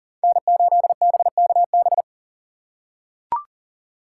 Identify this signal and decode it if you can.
Morse code, heard as a keyed tone, about 37 wpm, 700 Hz. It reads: N8BKB
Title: USA Call Signs - 5 Characters - 2x Character Spacing - 40wpm